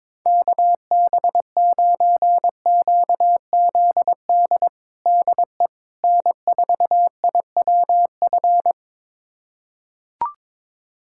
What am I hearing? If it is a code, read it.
Morse code, heard as a keyed tone, about 22 wpm, 700 Hz. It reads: KB9QZD DE N4IWF
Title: USA Callsign DE Callsign 22wpm